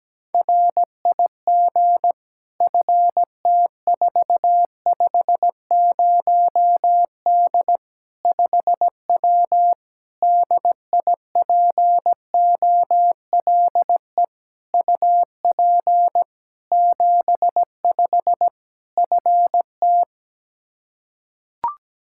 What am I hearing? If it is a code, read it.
Morse code, heard as a keyed tone, about 17 wpm, 700 Hz. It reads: RIG FT450D 5W DIPOLE UP 75 FT
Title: Practice QSO - Exchange 2 - Info - 17wpm